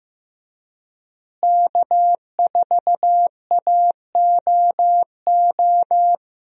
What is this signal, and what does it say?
Morse code, heard as a keyed tone, about 15 wpm, 700 Hz. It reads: K4AOO
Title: USA Call Signs - 5 Characters - Repeated 3x - 15wpm